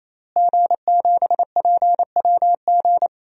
Morse code, heard as a keyed tone, about 28 wpm, 700 Hz. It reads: G7PWG
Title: DXCC-100 Call Signs - 5 Characters - Repeated 3x - 28wpm